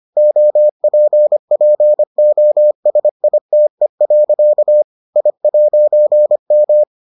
Morse code, heard as a keyed tone, about 25 wpm, 600 Hz. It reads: OPPOSITE. I'M